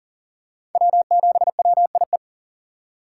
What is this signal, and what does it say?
Morse code, heard as a keyed tone, about 40 wpm, 700 Hz. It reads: W7WIE
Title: USA Call Signs - 5 Characters - 2x Character Spacing - 40wpm